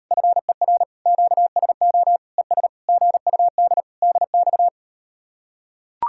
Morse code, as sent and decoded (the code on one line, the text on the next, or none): ..-. .-. / --.- ... --- / . ... / --. ..- -.. / -.. -..-
FER QSO ES GUD DX